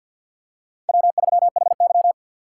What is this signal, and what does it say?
Morse code, heard as a keyed tone, about 50 wpm, 700 Hz. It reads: W3HY
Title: USA Call Signs - 2x Character Spacing - 50wpm